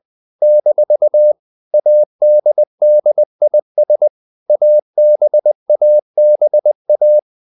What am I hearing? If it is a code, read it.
Morse code, heard as a keyed tone, about 20 wpm, 600 Hz. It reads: - ADDIS ABABA
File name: CW / Practice / NationalCapitals20